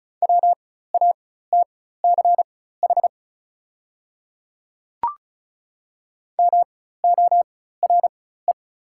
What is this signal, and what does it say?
Morse code, heard as a keyed tone, about 34 wpm, spaced as for 16 wpm, 700 Hz. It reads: WATCH MORE
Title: 4x Character Spacing - Top 300 Words - 35wpm